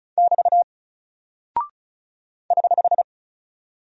Morse code, heard as a keyed tone, about 36 wpm, 700 Hz. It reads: <BT> <HH>
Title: Single Character (Letter:Number:Punctuation:Prosign) 35wpm